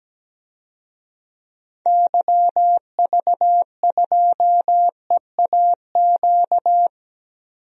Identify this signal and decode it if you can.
Morse code, heard as a keyed tone, about 17 wpm, 700 Hz. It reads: YV2EAQ
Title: DXCC-100 Call Signs - 6 Characters - 17wpm